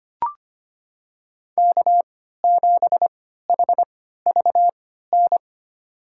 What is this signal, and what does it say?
Morse code, heard as a keyed tone, about 25 wpm, spaced as for 14 wpm, 700 Hz. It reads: K7HVN
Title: USA Call Signs - 5 Characters - 3x Character Spacing - 25wpm